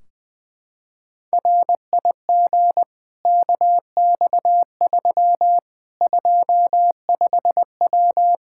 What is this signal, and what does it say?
Morse code, heard as a keyed tone, about 20 wpm, 700 Hz. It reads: RIG KX3 25W
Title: Practice QSO - Exchange 2 - Info - 20wpm